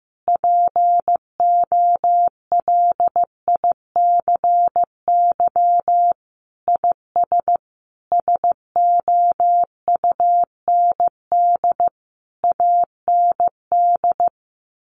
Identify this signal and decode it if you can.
Morse code, heard as a keyed tone, about 15 wpm, 700 Hz. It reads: POLICY IS SOUND AND
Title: Sentences from Top 1400 Words 15wpm